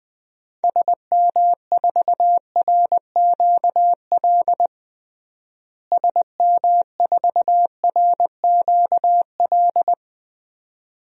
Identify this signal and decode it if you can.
Morse code, heard as a keyed tone, about 20 wpm, 700 Hz. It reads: SM4RQL SM4RQL
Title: DXCC-100 Call Signs - 6 Characters - Repeated 2x - 20wpm